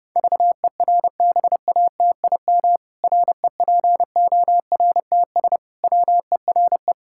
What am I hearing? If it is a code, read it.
Morse code, heard as a keyed tone, about 32 wpm, 700 Hz. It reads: VERBATIM REPORTS WERE
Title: Pangram Challenge v2 - 30wpm